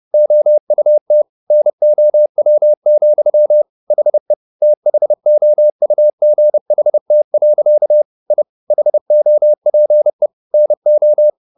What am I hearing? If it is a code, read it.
Morse code, heard as a keyed tone, about 30 wpm, 600 Hz. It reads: OUT NOW, HE THOUGHT. I HOPE NO